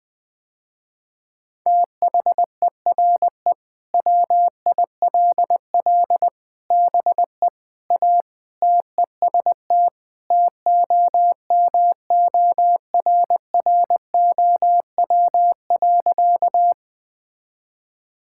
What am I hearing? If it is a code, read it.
Morse code, heard as a keyed tone, about 20 wpm, 700 Hz. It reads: THERE WILL BE A TEST TOMORROW.
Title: Sentences with 3-Word Phrases - Spoken Beginning - 20wpm